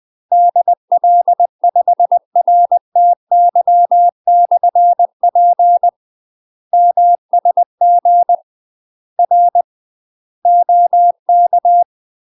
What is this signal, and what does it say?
Morse code, heard as a keyed tone, about 20 wpm, 700 Hz. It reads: DL5RTY/P MSG R OK
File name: UK MASTER QSOs TRAINING / 20WPM / UK MASTER QSOs 0004